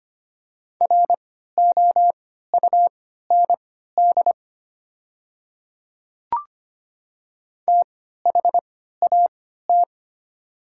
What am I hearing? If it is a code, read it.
Morse code, heard as a keyed tone, about 25 wpm, spaced as for 14 wpm, 700 Hz. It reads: ROUND THAT
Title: 3x Character Spacing - Top 200 Words - 25wpm